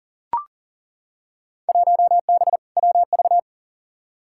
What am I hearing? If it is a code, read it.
Morse code, heard as a keyed tone, about 40 wpm, 700 Hz. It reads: 1B WV
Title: Field Day Exchanges Encore - 40wpm